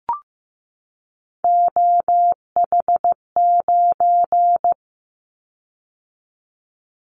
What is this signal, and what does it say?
Morse code, heard as a keyed tone, about 15 wpm, 700 Hz. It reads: OH9
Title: Callsign Prefixes for Top 50 DXCC - 15wpm